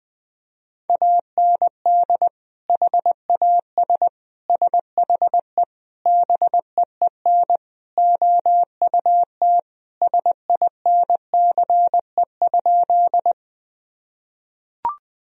Morse code, heard as a keyed tone, about 21 wpm, 700 Hz. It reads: AND HAS SHE BEEN OUT SINCE?
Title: Sentences from Top 1000 Words - Review Part 1 - 20wpm